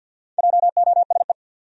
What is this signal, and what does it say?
Morse code, heard as a keyed tone, about 51 wpm, 700 Hz. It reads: JOSE
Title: USA Common Names - 50wpm